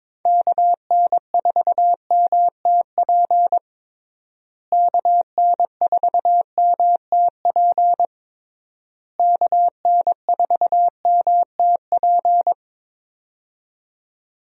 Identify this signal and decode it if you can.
Morse code, heard as a keyed tone, about 22 wpm, 700 Hz. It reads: KN4MTP KN4MTP KN4MTP
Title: USA Call Signs - 6 Characters - Repeated 3x - 22wpm